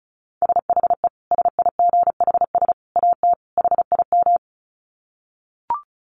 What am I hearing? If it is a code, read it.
Morse code, heard as a keyed tone, about 34 wpm, 700 Hz. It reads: SHE SIGHS AT HIM
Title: Dit Challenge 35wpm